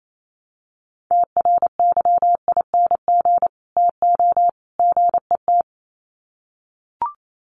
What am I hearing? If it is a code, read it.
Morse code, heard as a keyed tone, about 28 wpm, 700 Hz. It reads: TRYING TO GET
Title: Sets of 3 Words 28wpm